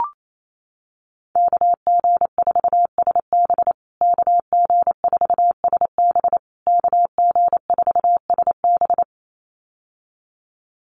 Morse code, heard as a keyed tone, about 28 wpm, 700 Hz. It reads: KG4SB KG4SB KG4SB
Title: USA Call Signs Repeated 3x 28wpm